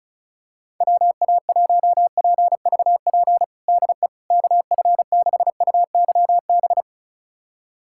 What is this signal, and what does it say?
Morse code, heard as a keyed tone, about 34 wpm, 690 Hz. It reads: WA1PVP DE KF6UYB
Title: USA Callsign DE Callsign - 6 Characters - 35wpm